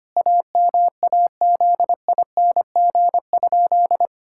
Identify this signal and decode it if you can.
Morse code, heard as a keyed tone, about 25 wpm, 700 Hz. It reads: AMAZING?
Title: Pangram Challenge v2 - 25wpm